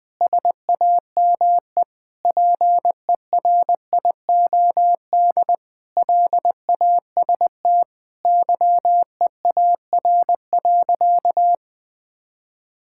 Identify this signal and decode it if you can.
Morse code, heard as a keyed tone, about 21 wpm, 700 Hz. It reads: SAME PERIOD LAST YEAR.
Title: Sentences with 7-Word Phrases - 20wpm